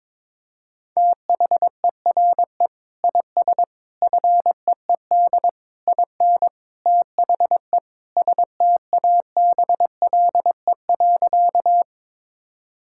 Morse code, heard as a keyed tone, about 22 wpm, 700 Hz. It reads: THERE IS FEED IN THE STABLE.
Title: Sentences from Top 1600 Words 22wpm